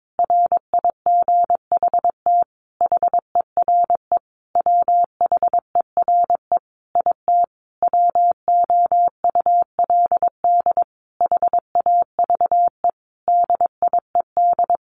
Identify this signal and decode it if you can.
Morse code, heard as a keyed tone, about 22 wpm, 700 Hz. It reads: RIGHT HERE WHERE IT WOULD HAVE DIED
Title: A Sound Of Thunder 22wpm